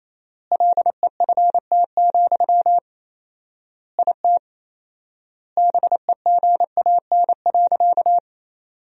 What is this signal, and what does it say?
Morse code, heard as a keyed tone, about 27 wpm, 700 Hz. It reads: LEFT, IT BEGAN.
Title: 4x Word-Spacing - Sentences from Top 400 Words - 28wpm